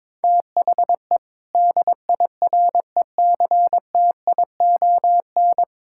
Morse code, heard as a keyed tone, about 22 wpm, 700 Hz. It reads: THE DIRECTION
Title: Sentences from Top 1000 Words - Review Part 5 - 22wpm